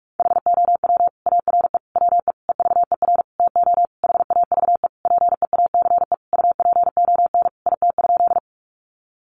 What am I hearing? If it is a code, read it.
Morse code, heard as a keyed tone, about 44 wpm, 700 Hz. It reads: HOW ARE WE EVER TO HAVE PEACE UPON IT?
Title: Sentences from Top 1500 Words 45wpm